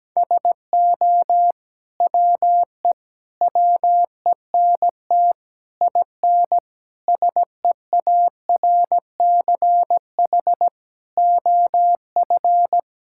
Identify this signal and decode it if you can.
Morse code, heard as a keyed tone, about 17 wpm, 700 Hz. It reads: SO WE WENT IN SEARCH OF
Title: Sentences from Top 1000 Words - Review Part 5 - 17wpm